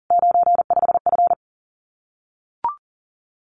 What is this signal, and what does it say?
Morse code, heard as a keyed tone, about 42 wpm, 700 Hz. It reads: 95F
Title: Common QSO Phrases Encore 40wpm